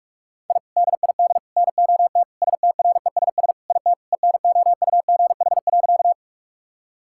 Morse code, heard as a keyed tone, about 44 wpm, 700 Hz. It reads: I DID NOT STRESS IT ENOUGH.
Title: Sentences from Top 1000 Words - Review Part 6 - 45wpm